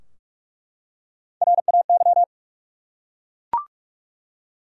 Morse code, heard as a keyed tone, about 47 wpm, 700 Hz. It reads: RAY
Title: USA Common Names - Max 4 Letters - 45wpm